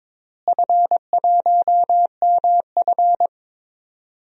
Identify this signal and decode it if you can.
Morse code, heard as a keyed tone, about 22 wpm, 700 Hz. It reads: F1MF
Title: DXCC-100 Call Signs - 4 Characters - Repeated 3x - 22wpm